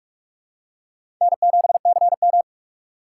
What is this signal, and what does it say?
Morse code, heard as a keyed tone, about 47 wpm, 700 Hz. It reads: N7CM
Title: USA Call Signs - 4 Characters - 3x Character Spacing - 45wpm